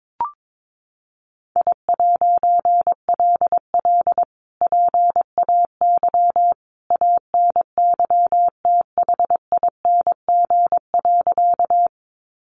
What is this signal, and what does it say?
Morse code, heard as a keyed tone, about 22 wpm, 700 Hz. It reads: I'LL PAY ANYTHING.
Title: A Sound Of Thunder 22wpm